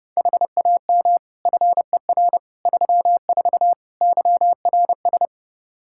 Morse code, heard as a keyed tone, about 30 wpm, 700 Hz. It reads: HAM FER 34 YRS
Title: Practice QSO - Exchange 3 - More Info - 30wpm